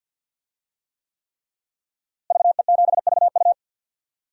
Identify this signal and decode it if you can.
Morse code, heard as a keyed tone, about 50 wpm, 700 Hz. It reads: VE7VU
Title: DXCC-100 Call Signs - 5 Characters - Repeated 3x - 50wpm